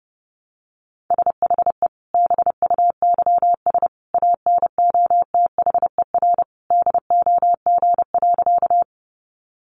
Morse code, heard as a keyed tone, about 30 wpm, 700 Hz. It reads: SHE BUYS ANOTHER DOG.
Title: Easy Phrases 30wpm